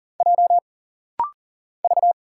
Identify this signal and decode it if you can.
Morse code, heard as a keyed tone, about 40 wpm, 700 Hz. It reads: J V
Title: Single Letters - Rapid-Fire 40wpm